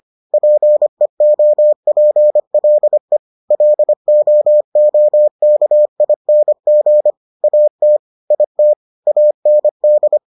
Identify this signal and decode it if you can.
Morse code, heard as a keyed tone, about 25 wpm, 600 Hz. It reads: PEOPLE LOOKING AT IT AND